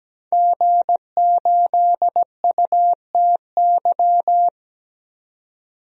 Morse code, heard as a keyed tone, about 17 wpm, 700 Hz. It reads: G8UTY
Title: DXCC-100 Call Signs - 5 Characters - 17wpm